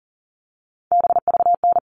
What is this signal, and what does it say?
Morse code, heard as a keyed tone, about 40 wpm, 700 Hz. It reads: BVN
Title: All Cut Numbers - 3-Digit 40wpm